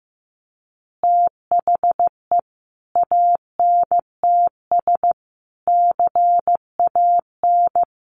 Morse code, heard as a keyed tone, about 14 wpm, 700 Hz. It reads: THE ANTS CAN
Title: Additional Easy Phrases 15wpm